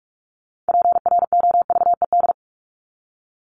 Morse code, heard as a keyed tone, about 44 wpm, 700 Hz. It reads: PROVED
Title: ED Suffix - Top 100 Words with Base Words 45wpm